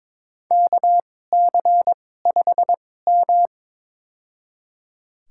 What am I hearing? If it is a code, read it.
Morse code, heard as a keyed tone, about 22 wpm, spaced as for 16 wpm, 700 Hz. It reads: KC5M
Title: USA Call Signs - 4 Characters - 2x Character Spacing - 22wpm